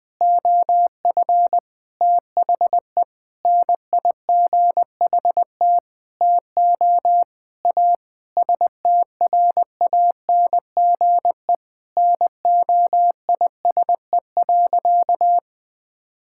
Morse code, heard as a keyed tone, about 20 wpm, 700 Hz. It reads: OF THE NIGHT TO A STRANGE NOISE.
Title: Sentences with 8-Word Phrases - 20wpm